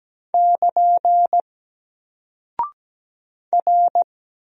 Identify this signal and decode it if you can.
Morse code, heard as a keyed tone, about 17 wpm, 700 Hz. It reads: <KN> R
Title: Single Character (Letter:Number:Punctuation:Prosign) 17wpm